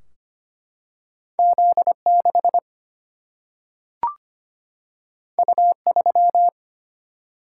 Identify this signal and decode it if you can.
Morse code, heard as a keyed tone, about 25 wpm, 700 Hz. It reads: Z6 U3
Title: Single-Letter Single-Number 25wpm